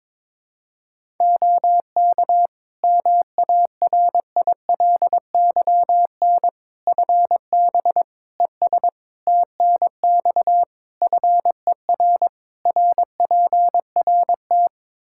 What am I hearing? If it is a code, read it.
Morse code, heard as a keyed tone, about 21 wpm, 700 Hz. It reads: OK MARILYN FB ES TNX FER RPRT
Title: Practice QSO - Exchange 2 - Info - 22wpm